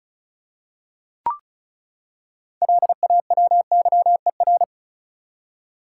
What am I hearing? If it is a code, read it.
Morse code, heard as a keyed tone, about 36 wpm, 700 Hz. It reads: LAWYER
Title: Six Letter Words - 35wpm